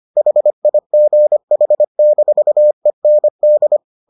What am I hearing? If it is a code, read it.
Morse code, heard as a keyed tone, about 25 wpm, 600 Hz. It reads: HIGH-END